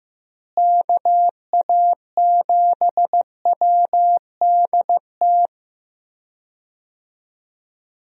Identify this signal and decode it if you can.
Morse code, heard as a keyed tone, about 15 wpm, 700 Hz. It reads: KA7WDT
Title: USA Call Signs - 6 Characters - Repeated 3x - 15wpm